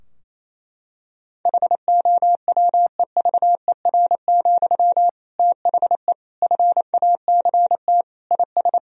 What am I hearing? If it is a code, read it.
Morse code, heard as a keyed tone, about 28 wpm, 700 Hz. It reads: HOWEVER, THE FACT IS
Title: Sentences from Top 1000 Words - Review Part 3 - 28wpm